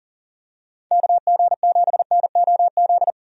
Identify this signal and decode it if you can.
Morse code, heard as a keyed tone, about 40 wpm, 700 Hz. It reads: KG7NOZ
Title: USA Call Signs - 6 Characters - 40wpm